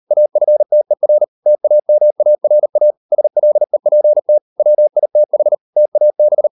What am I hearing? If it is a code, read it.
Morse code, heard as a keyed tone, about 38 wpm, 600 Hz. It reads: AFTER TAMARA SLEPT WITH TAB